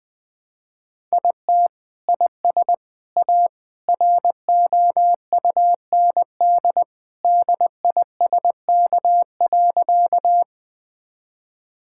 Morse code, heard as a keyed tone, about 20 wpm, 700 Hz. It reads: IT IS A ROUND DISK.
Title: Sentences from Top 1000 Words - Review Part 6 - 20wpm